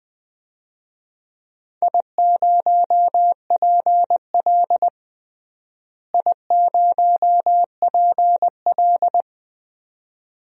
Morse code, heard as a keyed tone, about 20 wpm, 700 Hz. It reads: I0PL I0PL
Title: DXCC-100 Call Signs - 4 Characters - Repeated 2x - 20wpm